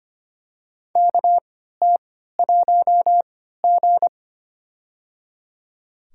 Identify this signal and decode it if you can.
Morse code, heard as a keyed tone, about 25 wpm, spaced as for 14 wpm, 700 Hz. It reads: KT1G
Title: USA Call Signs - 3x Character Spacing - 25wpm